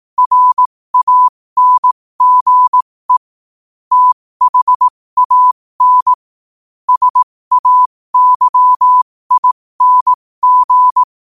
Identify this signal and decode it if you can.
Morse code, heard as a keyed tone, about 18 wpm, spaced as for 15 wpm, 1000 Hz. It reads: RANGE THAN SAYING
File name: qotdfiles / 15wpm